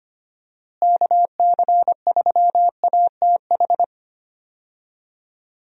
Morse code, heard as a keyed tone, about 25 wpm, 700 Hz. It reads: KC3ATH
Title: USA Call Signs - 6 Characters - Repeated 2x - 25wpm